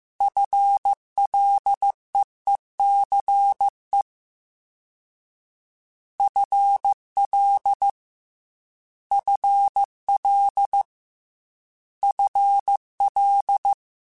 Morse code, heard as a keyed tone, about 15 wpm, 800 Hz. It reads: FLEECE FL FL FL
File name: FL / FL mp3s / FL Prefixes 3x2 @13